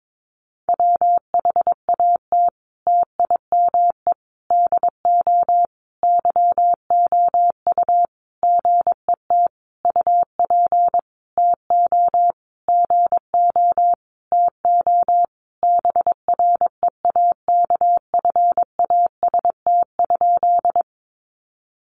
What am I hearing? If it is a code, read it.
Morse code, heard as a keyed tone, about 22 wpm, 700 Hz. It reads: WHAT TIME DO YOU GET UP TO GO TO BREAKFAST?
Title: Sentences from Top 1500 Words 22wpm